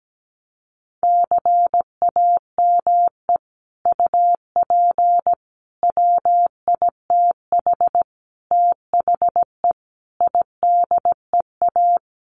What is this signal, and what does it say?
Morse code, heard as a keyed tone, about 17 wpm, 700 Hz. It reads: CAME UP WITH THE IDEA
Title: Sets of 5 Words Encore2 17wpm